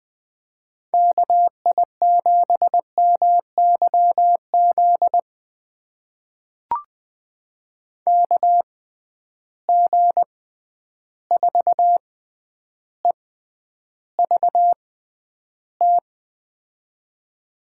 Morse code, heard as a keyed tone, about 21 wpm, spaced as for 10 wpm, 700 Hz. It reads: KI7MYZ KG4EVT
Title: USA Call Signs - 6 Characters - 6x Character Spacing - 20wpm